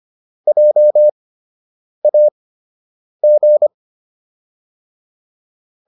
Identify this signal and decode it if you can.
Morse code, heard as a keyed tone, about 25 wpm, spaced as for 8 wpm, 600 Hz. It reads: JAG